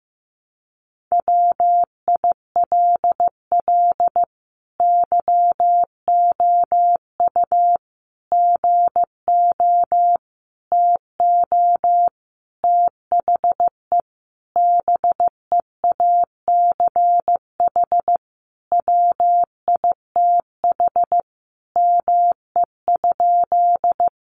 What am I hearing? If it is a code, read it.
Morse code, heard as a keyed tone, about 15 wpm, 700 Hz. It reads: WILL YOU GO TO THE BEACH WITH ME?
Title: Sentences from Top 1500 Words 15wpm